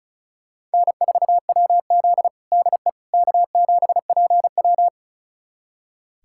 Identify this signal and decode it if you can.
Morse code, heard as a keyed tone, about 36 wpm, 700 Hz. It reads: N4WZ DE K7PW
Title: USA Callsign DE Callsign - 4 Characters - 35wpm